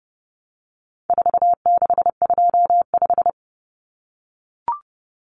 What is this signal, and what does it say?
Morse code, heard as a keyed tone, about 28 wpm, 700 Hz. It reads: 4625
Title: Common Cut Numbers - 4-Digit 30wpm